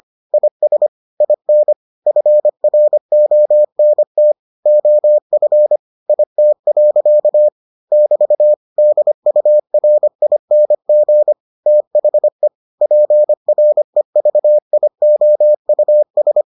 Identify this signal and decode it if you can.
Morse code, heard as a keyed tone, about 25 wpm, 600 Hz. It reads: IS IN FRONT OF IT. <BT> DURING THE PREVIOUS